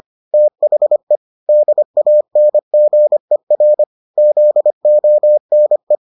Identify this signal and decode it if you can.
Morse code, heard as a keyed tone, about 25 wpm, 600 Hz. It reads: THE DANGER ZONE